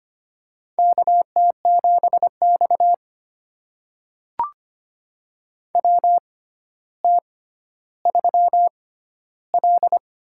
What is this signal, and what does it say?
Morse code, heard as a keyed tone, about 25 wpm, spaced as for 13 wpm, 700 Hz. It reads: KT7X WT3L
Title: USA Call Signs - 4 Characters - 6x Character Spacing - 25wpm